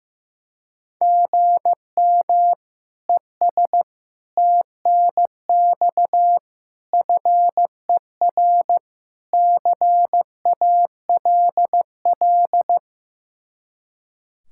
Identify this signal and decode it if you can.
Morse code, heard as a keyed tone, about 15 wpm, 700 Hz. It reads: GM ES TNX FER CALL
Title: Practice QSO - Exchange 1 - Beginning - 15wpm